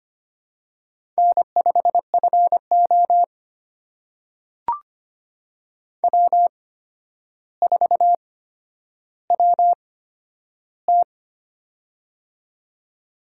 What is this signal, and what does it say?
Morse code, heard as a keyed tone, about 25 wpm, spaced as for 11 wpm, 700 Hz. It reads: N5FO W4WT
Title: USA Call Signs - 4 Characters - 8x Character Spacing - 25wpm